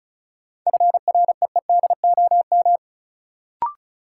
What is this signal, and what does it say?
Morse code, heard as a keyed tone, about 36 wpm, 700 Hz. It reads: FREEDOM
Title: Sentences from Top 800 Words - New Word Drill - 35wpm